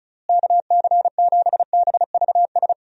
Morse code, heard as a keyed tone, about 34 wpm, 700 Hz. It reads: KC7BVS
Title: USA Call Signs - 6 Characters - Repeated 2x - 35wpm